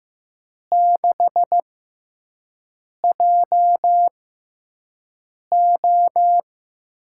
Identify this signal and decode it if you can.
Morse code, heard as a keyed tone, about 15 wpm, spaced as for 5 wpm, 700 Hz. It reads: 6JO
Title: USA Call Signs - 6 Characters - 6x Character Spacing - 15wpm